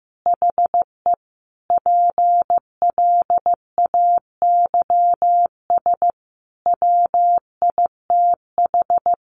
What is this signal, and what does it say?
Morse code, heard as a keyed tone, about 15 wpm, 700 Hz. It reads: HE PLAYS WITH
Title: Easy Phrases 15wpm